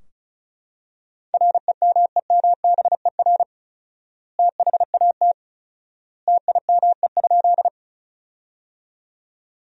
Morse code, heard as a keyed tone, about 36 wpm, 700 Hz. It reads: REMEMBER THAT TIME?
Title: 4x Word-Spacing - Sentences from Top 400 Words - 35wpm